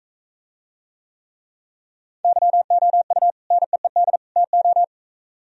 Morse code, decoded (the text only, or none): YOU NEED TO